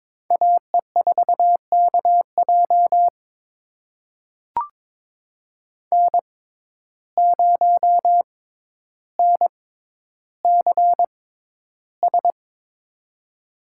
Morse code, heard as a keyed tone, about 23 wpm, spaced as for 11 wpm, 700 Hz. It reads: AE4KJ N0NCS
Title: USA Call Signs - 5 Characters - 6x Character Spacing - 22wpm